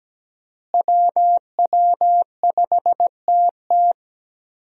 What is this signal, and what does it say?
Morse code, heard as a keyed tone, about 17 wpm, 700 Hz. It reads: WW5TT
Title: USA Call Signs - 5 Characters - Repeated 3x - 17wpm